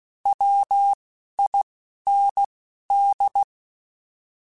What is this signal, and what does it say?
Morse code, heard as a keyed tone, about 16 wpm, 800 Hz. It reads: W I N D
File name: Alphabetical / Alphabetical mp3s / Ws @10